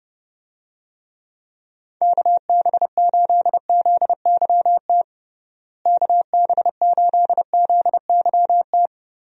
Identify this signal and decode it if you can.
Morse code, heard as a keyed tone, about 30 wpm, 700 Hz. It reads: KB8ZYT KB8ZYT
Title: USA Call Signs - 6 Characters - Repeated 3x - 30wpm